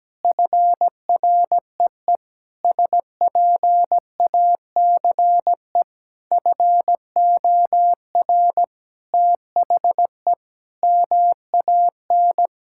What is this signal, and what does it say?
Morse code, heard as a keyed tone, about 17 wpm, 700 Hz. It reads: FREE SPACE FOR THE MAN
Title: Sentences from Top 1000 Words - Review Part 5 - 17wpm